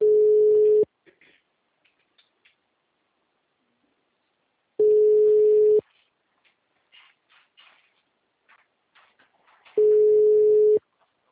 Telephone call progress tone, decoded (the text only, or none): ringback tone